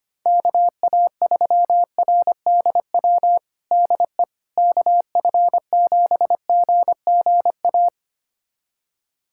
Morse code, heard as a keyed tone, about 25 wpm, 700 Hz. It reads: KA3RDW DE KF7GGA
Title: USA Callsign DE Callsign 25wpm